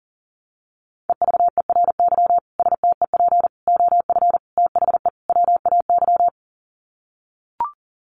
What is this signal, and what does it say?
Morse code, heard as a keyed tone, about 40 wpm, 700 Hz. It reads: EVERY STEP OF THE WAY
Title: Sets of 5 Words Encore2 40wpm